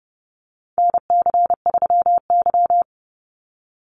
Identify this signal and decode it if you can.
Morse code, heard as a keyed tone, about 30 wpm, 700 Hz. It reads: NC3Y
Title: Call Signs - 30wpm